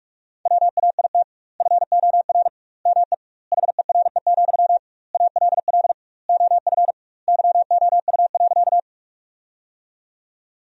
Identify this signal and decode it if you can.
Morse code, heard as a keyed tone, about 46 wpm, 700 Hz. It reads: WAIT FOR ME HERE, ALL OF YOU.
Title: 2x Word-Spacing - Sentences from Top 500 Words - 45wpm